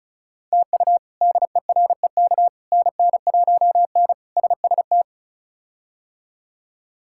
Morse code, heard as a keyed tone, about 34 wpm, 700 Hz. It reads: TU DEREK NN1N SST
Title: K1USN SST QSO Practice - 35wpm